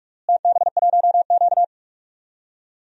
Morse code, heard as a keyed tone, about 45 wpm, 700 Hz. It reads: TB1Q
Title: DXCC-100 Call Signs - 4 Characters - Repeated 3x - 45wpm